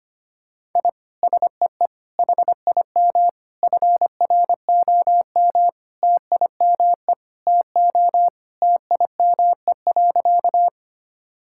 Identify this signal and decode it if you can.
Morse code, heard as a keyed tone, about 25 wpm, 700 Hz. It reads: I SEE HIM FROM TIME TO TIME.
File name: practice / 25wpm / Sentences with 4-Word Phrases - 25wpm